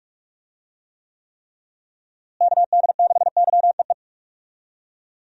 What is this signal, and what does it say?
Morse code, heard as a keyed tone, about 42 wpm, 700 Hz. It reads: KD6YEE
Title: USA Call Signs - 6 Characters - Repeated 3x - 45wpm